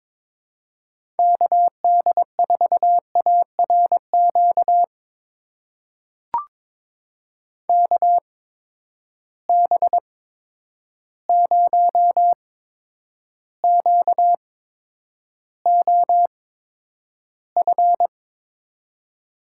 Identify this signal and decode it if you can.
Morse code, heard as a keyed tone, about 22 wpm, spaced as for 9 wpm, 700 Hz. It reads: KD4ARQ KB0QOF
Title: USA Call Signs - 6 Characters - 8x Character Spacing - 22wpm